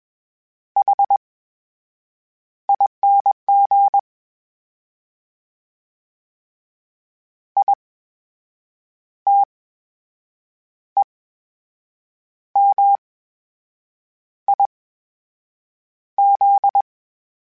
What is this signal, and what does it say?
Morse code, heard as a keyed tone, about 21 wpm, spaced as for 6 wpm, 800 Hz. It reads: HING ITEMIZ
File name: Special INGs-mp3s / Special INGs-mp3s / VerySlowToFaster Ing @20